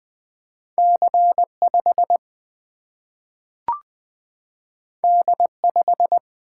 Morse code, heard as a keyed tone, about 20 wpm, 700 Hz. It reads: C5 D5
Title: Single-Letter Single-Number 20wpm